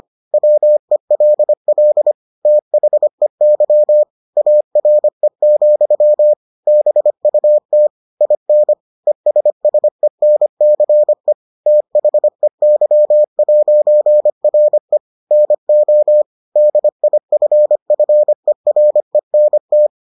Morse code, heard as a keyed tone, about 25 wpm, 600 Hz. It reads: WELL THEY ARE, BUT IN ESSENCE THEY'RE NO DIFFERENT